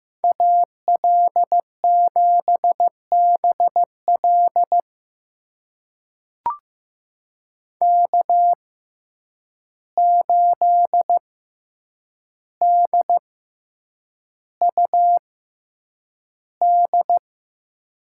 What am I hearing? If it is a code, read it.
Morse code, heard as a keyed tone, about 15 wpm, spaced as for 8 wpm, 700 Hz. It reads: AL7BL K8DUD
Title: USA Call Signs - 5 Characters - 6x Character Spacing - 15wpm